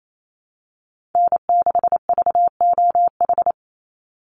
Morse code, heard as a keyed tone, about 28 wpm, 700 Hz. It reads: N6VOH
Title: Call Signs - 28wpm